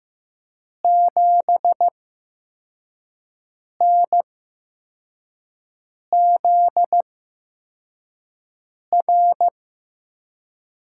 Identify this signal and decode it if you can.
Morse code, heard as a keyed tone, about 15 wpm, spaced as for 4 wpm, 700 Hz. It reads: 7NZR
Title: USA Call Signs - 5 Characters - 8x Character Spacing - 15wpm